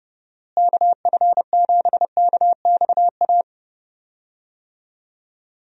Morse code, heard as a keyed tone, about 30 wpm, 700 Hz. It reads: KF7KXA
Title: USA Call Signs - 6 Characters - Repeated 3x - 30wpm